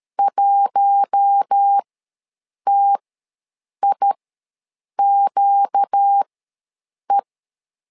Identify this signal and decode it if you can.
Morse code, heard as a keyed tone, about 13 wpm, spaced as for 7 wpm, 800 Hz. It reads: 1TIQE
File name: LEZIONI / 4 Q1 ws 12wpm 2129